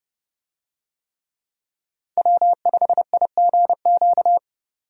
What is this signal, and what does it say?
Morse code, heard as a keyed tone, about 30 wpm, 700 Hz. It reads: W5IGQ
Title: USA Call Signs - 5 Characters - 30wpm